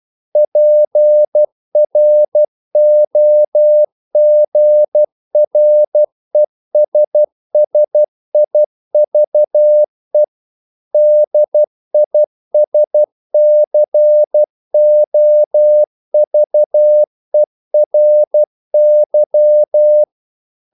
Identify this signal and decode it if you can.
Morse code, heard as a keyed tone, about 12 wpm, 600 Hz. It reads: PROGRESSIVE DISCOVERY
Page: English quotations in CW (telegraphy - morse)